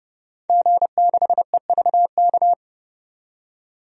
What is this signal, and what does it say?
Morse code, heard as a keyed tone, about 31 wpm, 700 Hz. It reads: G6EVK
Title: DXCC-100 Call Signs - 5 Characters - 30wpm